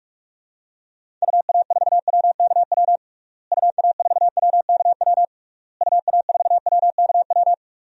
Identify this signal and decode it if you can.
Morse code, heard as a keyed tone, about 46 wpm, 700 Hz. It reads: UA4WKW UA4WKW UA4WKW
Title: DXCC-100 Call Signs - 6 Characters - Repeated 3x - 45wpm